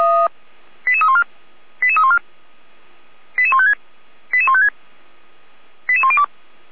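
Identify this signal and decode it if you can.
Five-tone selective call (ZVEI), heard as quick runs of five tones, tones of about 80 ms. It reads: 88314 88314 88156 88156 88102
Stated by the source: ZVEI five-tone selective call 88279, received on an SDR.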